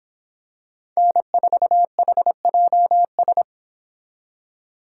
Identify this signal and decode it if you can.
Morse code, heard as a keyed tone, about 26 wpm, 700 Hz. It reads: N4HJS